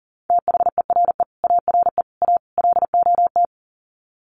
Morse code, heard as a keyed tone, about 40 wpm, 700 Hz. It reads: THERE ARE A LOT
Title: Sets of 4 Words 40wpm